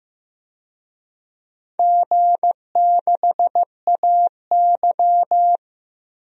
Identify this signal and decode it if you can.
Morse code, heard as a keyed tone, about 15 wpm, 700 Hz. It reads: G6AY
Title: DXCC-100 Call Signs - 4 Characters - Repeated 3x - 15wpm